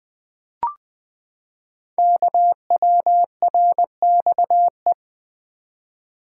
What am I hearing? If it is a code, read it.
Morse code, heard as a keyed tone, about 20 wpm, 700 Hz. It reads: KWRXE
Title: Code Groups - 5 Letters - 20wpm